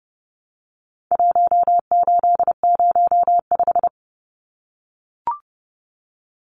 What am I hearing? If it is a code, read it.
Morse code, heard as a keyed tone, about 30 wpm, 700 Hz. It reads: 1805
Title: Numbers - Contest Style - 4-Digit 30wpm